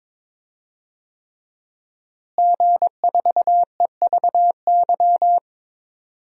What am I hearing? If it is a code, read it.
Morse code, heard as a keyed tone, about 22 wpm, 700 Hz. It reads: G4EVY
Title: DXCC-100 Call Signs - 5 Characters - Repeated 3x - 22wpm